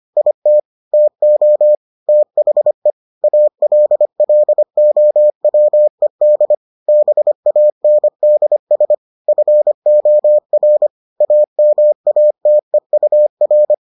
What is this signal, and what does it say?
Morse code, heard as a keyed tone, about 25 wpm, 600 Hz. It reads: IT TO THE ALLOWED BANDS FOR AMATEUR